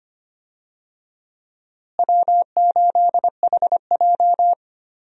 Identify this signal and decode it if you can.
Morse code, heard as a keyed tone, about 25 wpm, 700 Hz. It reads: W8HJ
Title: USA Call Signs - 4 Characters - Repeated 2x - 25wpm